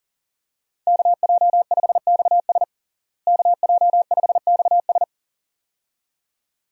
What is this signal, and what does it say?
Morse code, heard as a keyed tone, about 40 wpm, 700 Hz. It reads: KJ5XS KJ5XS
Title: USA Call Signs - 5 Characters - Repeated 2x - 40wpm